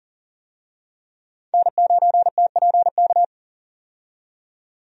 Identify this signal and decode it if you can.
Morse code, heard as a keyed tone, about 40 wpm, 700 Hz. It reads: N9TPK
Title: USA Call Signs - 5 Characters - 40wpm